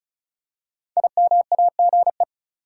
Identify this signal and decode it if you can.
Morse code, heard as a keyed tone, about 34 wpm, 700 Hz. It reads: IMAGE
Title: Five Letter Words - 35wpm